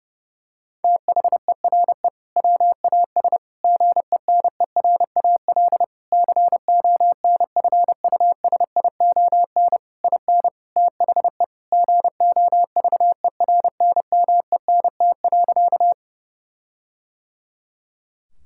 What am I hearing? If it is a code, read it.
Morse code, heard as a keyed tone, about 32 wpm, 700 Hz. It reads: THERE WAS GENERAL CONFUSION IN THE GOVERNMENT.
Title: Sentences from Top 1600 Words 30wpm